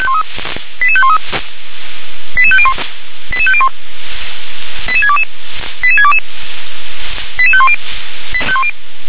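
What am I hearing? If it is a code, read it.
Five-tone selective call (ZVEI), heard as quick runs of five tones, tones of about 70 ms. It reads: 88513 88501 88501 88622 88622 88510
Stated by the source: Received on an SDR as a ZVEI five-tone call.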